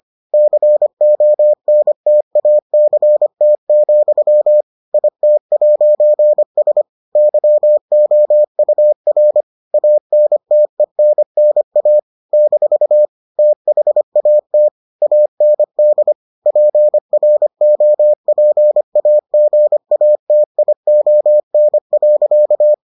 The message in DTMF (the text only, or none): none